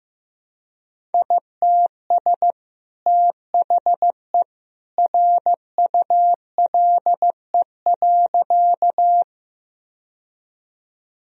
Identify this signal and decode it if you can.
Morse code, heard as a keyed tone, about 14 wpm, 700 Hz. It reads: ITS THE RULE.
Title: Sentences from Top 1000 Words - Review Part 2 - 15wpm